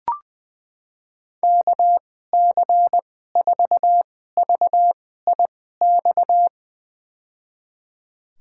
Morse code, heard as a keyed tone, about 20 wpm, spaced as for 15 wpm, 700 Hz. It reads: KC4VIX
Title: USA Call Signs - 6 Characters - 2x Character Spacing - 20wpm